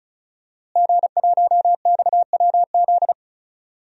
Morse code, heard as a keyed tone, about 36 wpm, 690 Hz. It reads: G1XWZ
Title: DXCC-100 Call Signs - 5 Characters - Repeated 3x - 35wpm